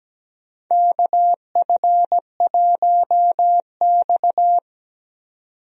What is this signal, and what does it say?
Morse code, heard as a keyed tone, about 17 wpm, 700 Hz. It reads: KF1X
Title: USA Call Signs - 4 Characters - 17wpm